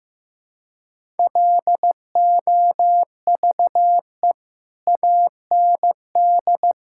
Morse code, heard as a keyed tone, about 15 wpm, 700 Hz. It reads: LOVE AND
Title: Sentences from Top 1000 Words - Review Part 5 - 15wpm